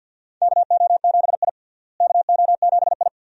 Morse code, heard as a keyed tone, about 51 wpm, 700 Hz. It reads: KO7I KO7I
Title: USA Call Signs - 4 Characters - Repeated 2x - 50wpm